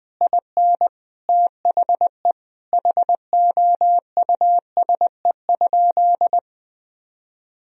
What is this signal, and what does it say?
Morse code, heard as a keyed tone, about 20 wpm, 700 Hz. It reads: IN THE HOUSE?
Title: Sentences from Top 1000 Words - Review Part 4 - 20wpm